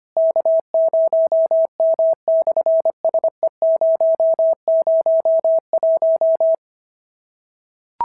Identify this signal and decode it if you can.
Morse code, heard as a keyed tone, about 24 wpm, 650 Hz. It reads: K0M/SE001
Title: SOTA QSO Practice 25wpm